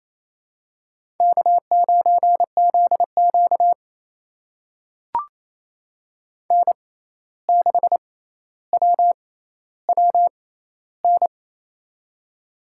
Morse code, heard as a keyed tone, about 28 wpm, spaced as for 13 wpm, 700 Hz. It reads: K9ZQ N6WWN
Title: USA Call Signs - 6x Character Spacing - 28wpm